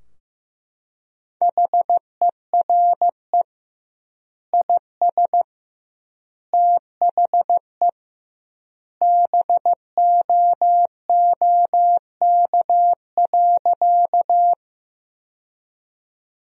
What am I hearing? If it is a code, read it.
Morse code, heard as a keyed tone, about 16 wpm, 700 Hz. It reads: HERE IS THE BOOK.
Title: 2x Word-Spacing - Sentences from Top 300 Words - 15wpm